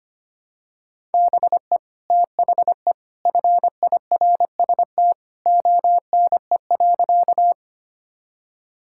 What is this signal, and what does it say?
Morse code, heard as a keyed tone, about 25 wpm, 690 Hz. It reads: BE THE FIRST ONE.
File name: practice / 25wpm / Sentences with 2-Word Phrases - 25wpm